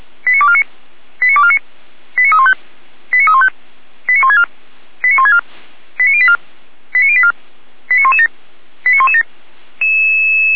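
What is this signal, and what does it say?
Five-tone selective call (ZVEI), heard as quick runs of five tones, tones of about 70 ms. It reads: 89249 89249 89315 89315 89164 89164 89084 89084 89117 89117
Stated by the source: ZVEI five-tone call 89280 received on SDR.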